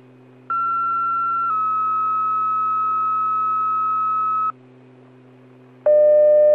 Two-tone page, heard as a one-second tone, then a three-second tone, about 1360 then 1290 Hz. It is heard from 0.5 to 4.5 s.